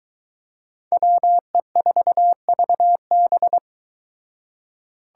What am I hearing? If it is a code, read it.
Morse code, heard as a keyed tone, about 22 wpm, 700 Hz. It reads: WE4VB